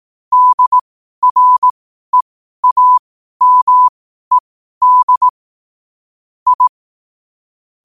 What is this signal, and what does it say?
Morse code, heard as a keyed tone, about 18 wpm, spaced as for 12 wpm, 1000 Hz. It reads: DREAMED I
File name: qotdfiles / 12wpm